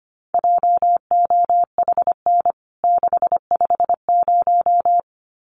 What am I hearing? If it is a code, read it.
Morse code, heard as a keyed tone, about 25 wpm, 700 Hz. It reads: JOHN 650
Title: CWT Exchange 25wpm